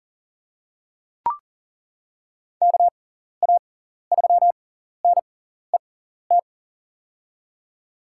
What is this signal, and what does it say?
Morse code, heard as a keyed tone, about 40 wpm, spaced as for 14 wpm, 700 Hz. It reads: KA3NET
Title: USA Call Signs - 6 Characters - 6x Character Spacing - 40wpm